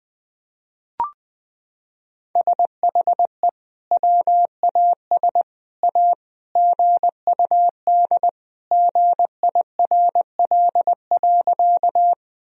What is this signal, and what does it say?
Morse code, heard as a keyed tone, about 21 wpm, 700 Hz. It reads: SHE WAS A GUD GIRL.
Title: Sentences from Top 400 Words and Abbreviations 20wpm